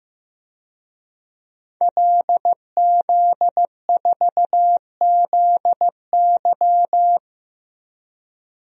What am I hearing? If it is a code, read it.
Morse code, heard as a keyed tone, about 15 wpm, 700 Hz. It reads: LZ4ZY
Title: DXCC-100 Call Signs - 5 Characters - 15wpm